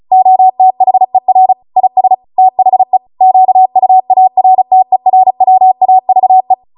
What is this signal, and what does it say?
Morse code, heard as a keyed tone, about 36 wpm, 750 Hz. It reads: OTHER IS THE QUARTERWAVE